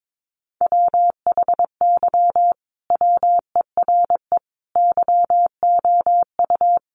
Morse code, heard as a keyed tone, about 22 wpm, 700 Hz. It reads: WHY WERE YOU
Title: Sentences from Top 1400 Words 22wpm